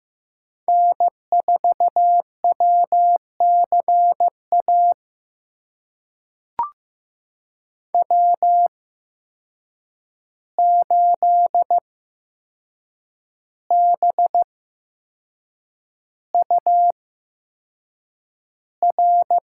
Morse code, heard as a keyed tone, about 15 wpm, spaced as for 6 wpm, 700 Hz. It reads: N4WCA W8BUR
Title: USA Call Signs - 5 Characters - 8x Character Spacing - 15wpm